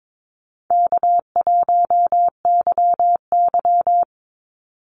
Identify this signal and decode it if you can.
Morse code, heard as a keyed tone, about 22 wpm, 700 Hz. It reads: K1YY
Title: Call Signs - 22wpm